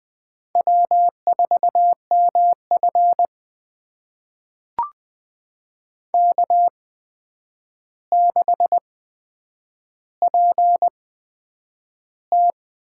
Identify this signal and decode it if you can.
Morse code, heard as a keyed tone, about 20 wpm, spaced as for 9 wpm, 700 Hz. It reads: W4MF K6PT
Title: USA Call Signs - 4 Characters - 8x Character Spacing - 20wpm